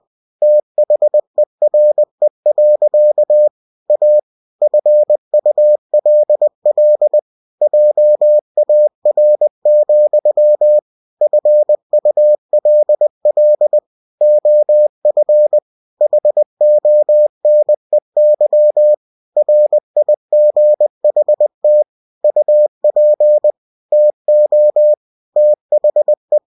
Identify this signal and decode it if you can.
Morse code, heard as a keyed tone, about 20 wpm, 600 Hz. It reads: THERE. A FULL JAR, FULL OF HONEY RIGHT UP TO THE
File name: cw-books / winnie-the-pooh / winnie-20wpm-page-0039